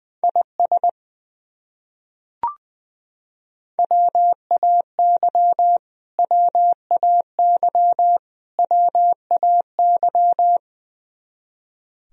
Morse code, heard as a keyed tone, about 20 wpm, 700 Hz. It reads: IS WAY WAY WAY
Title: Top 100 Words Repeated 20wpm